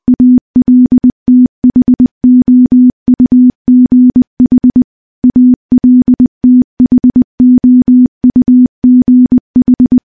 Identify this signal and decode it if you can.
Morse code, heard as a keyed tone, about 20 wpm, 250 Hz. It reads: ALTHOUGH ALTHOUGH
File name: morse-training-20WPM